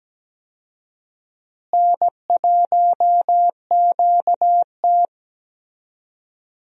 Morse code, heard as a keyed tone, about 17 wpm, 700 Hz. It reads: N1QT
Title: USA Call Signs - 4 Characters - Repeated 2x - 17wpm